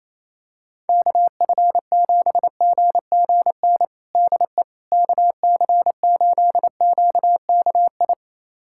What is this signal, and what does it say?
Morse code, heard as a keyed tone, about 27 wpm, 700 Hz. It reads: KF7GGN DE KC8QKI
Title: USA Callsign DE Callsign 28wpm